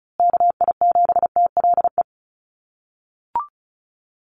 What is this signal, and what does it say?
Morse code, heard as a keyed tone, about 34 wpm, 700 Hz. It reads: KI7TLE
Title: USA Call Signs Repeated 2x 35wpm